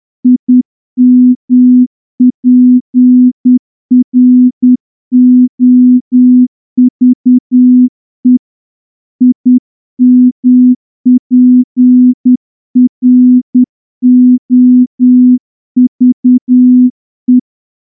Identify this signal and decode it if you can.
Morse code, heard as a keyed tone, about 9 wpm, 250 Hz. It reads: IMPROVE IMPROVE
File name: morse-training-10WPM